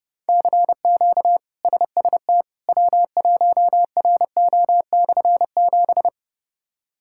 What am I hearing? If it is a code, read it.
Morse code, heard as a keyed tone, about 30 wpm, 700 Hz. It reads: CQ SST W1RO/7
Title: K1USN SST QSO Practice - 30wpm